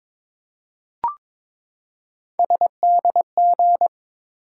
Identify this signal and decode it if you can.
Morse code, heard as a keyed tone, about 22 wpm, 700 Hz. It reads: SDG
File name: ARRL Sections v2 - 22wpm